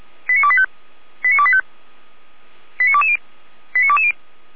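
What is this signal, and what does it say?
Five-tone selective call (ZVEI), heard as quick runs of five tones, tones of about 70 ms. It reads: 89285 89285 89220 89220
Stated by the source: ZVEI five-tone call received by SDR, recorded by Feuerwehr Orlamünde (f85360).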